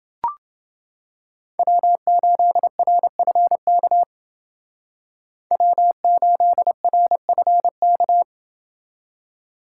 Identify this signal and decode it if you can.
Morse code, heard as a keyed tone, about 28 wpm, 700 Hz. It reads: W8RFK W8RFK